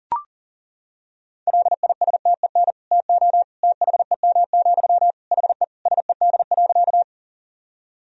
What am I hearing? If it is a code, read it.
Morse code, heard as a keyed tone, about 40 wpm, 700 Hz. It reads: LISTEN TO THEM, HE SED.
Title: Sentences from Top 400 Words and Abbreviations 40wpm